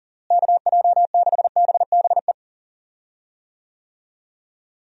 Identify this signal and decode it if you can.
Morse code, heard as a keyed tone, about 40 wpm, 700 Hz. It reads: KJ6BBE
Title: USA Call Signs - 6 Characters - Repeated 3x - 40wpm